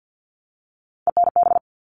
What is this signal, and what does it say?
Morse code, heard as a keyed tone, about 50 wpm, 700 Hz. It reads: ENB
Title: All Cut Numbers - 3-Digit 50wpm